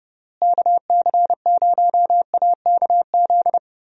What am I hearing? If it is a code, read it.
Morse code, heard as a keyed tone, about 31 wpm, 700 Hz. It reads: KC0AKZ
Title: USA Call Signs - 6 Characters - Repeated 2x - 30wpm